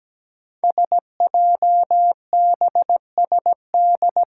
Morse code, heard as a keyed tone, about 17 wpm, 700 Hz. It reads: SJBSD
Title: Five Challenging Letter-Number Combinations 17wpm